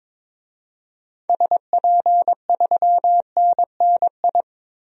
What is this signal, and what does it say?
Morse code, heard as a keyed tone, about 22 wpm, 700 Hz. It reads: SP3NNI
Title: DXCC-100 Call Signs - 6 Characters - 22wpm